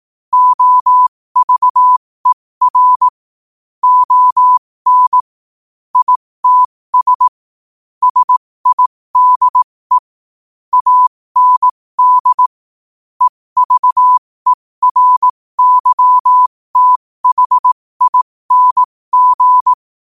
Morse code, heard as a keyed tone, about 18 wpm, spaced as for 15 wpm, 1000 Hz. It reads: OVER ON ITS SIDE AND EVERYTHING
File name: qotdfiles / 15wpm